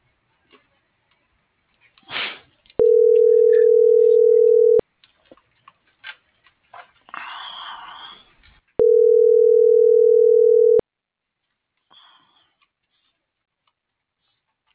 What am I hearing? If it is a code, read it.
Ringback tone (North America), 440 + 480 Hz.